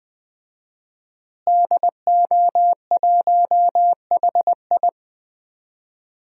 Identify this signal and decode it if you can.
Morse code, heard as a keyed tone, about 20 wpm, 700 Hz. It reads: DO1HI
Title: DXCC-100 Call Signs - 5 Characters - Repeated 2x - 20wpm